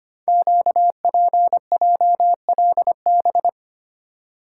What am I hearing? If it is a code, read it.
Morse code, heard as a keyed tone, about 25 wpm, 700 Hz. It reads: QPJLB
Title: Code Groups - 5 Letters - 25wpm